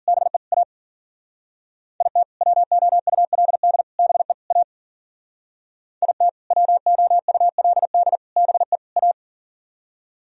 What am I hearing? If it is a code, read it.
Morse code, heard as a keyed tone, about 52 wpm, 700 Hz. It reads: BE A IT WOULD BE A IT WOULD BE A